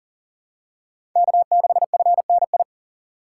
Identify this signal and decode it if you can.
Morse code, heard as a keyed tone, about 40 wpm, 700 Hz. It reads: K6FNI
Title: USA Call Signs - 5 Characters - 3x Character Spacing - 40wpm